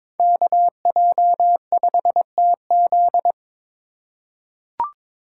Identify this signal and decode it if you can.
Morse code, heard as a keyed tone, about 22 wpm, 700 Hz. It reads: KJ5TZ
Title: USA Call Signs - 5 Characters - 2x Character Spacing - 22wpm